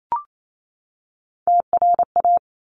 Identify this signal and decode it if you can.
Morse code, heard as a keyed tone, about 28 wpm, 700 Hz. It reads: TRA
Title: Top Three Letter Combinations in Words 28wpm